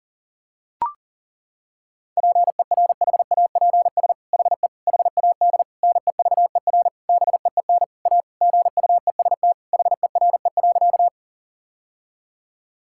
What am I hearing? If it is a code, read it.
Morse code, heard as a keyed tone, about 40 wpm, 700 Hz. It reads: PERHAPS HE HAD NEVER BEEN A GUEST HERE.
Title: Sentences from Top 1600 Words 40wpm